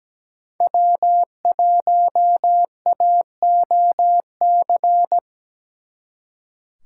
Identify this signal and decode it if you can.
Morse code, heard as a keyed tone, about 20 wpm, 700 Hz. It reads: W1AOC
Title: Speed Racing - USA 1x3 Call Signs 17wpm